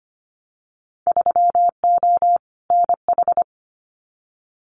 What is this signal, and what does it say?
Morse code, heard as a keyed tone, about 25 wpm, 700 Hz. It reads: 3O NH
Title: Winter Field Day Exchanges Encore 25wpm